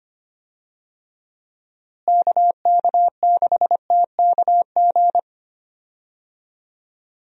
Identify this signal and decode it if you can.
Morse code, heard as a keyed tone, about 25 wpm, 700 Hz. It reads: KK6TKG
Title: USA Call Signs - 6 Characters - Repeated 3x - 25wpm